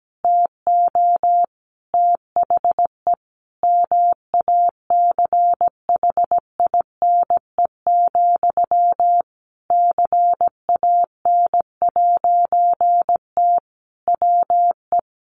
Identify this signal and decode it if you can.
Morse code, heard as a keyed tone, about 17 wpm, 700 Hz. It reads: TO THE MACHINE, CAN'T WE
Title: A Sound Of Thunder 17wpm